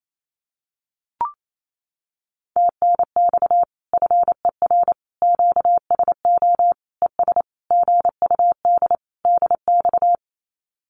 Morse code, heard as a keyed tone, about 28 wpm, 700 Hz. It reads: TNX FER QSO ES GUD DX
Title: Common QSO Phrases Encore 28wpm